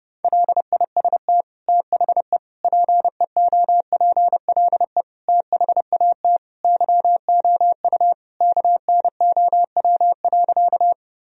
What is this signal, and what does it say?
Morse code, heard as a keyed tone, about 30 wpm, 700 Hz. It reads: LIST THE PEOPLE THAT YOU KNOW.
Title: Sentences from Top 1000 Words - Review Part 2 - 30wpm